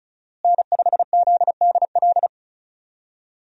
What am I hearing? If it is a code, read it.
Morse code, heard as a keyed tone, about 34 wpm, 700 Hz. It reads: N5ZDL
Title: USA Call Signs - 5 Characters - Repeated 3x - 35wpm